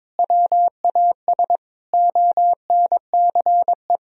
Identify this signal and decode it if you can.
Morse code, heard as a keyed tone, about 21 wpm, 700 Hz. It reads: WAS ONCE
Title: Sentences from Top 1000 Words - Review Part 3 - 22wpm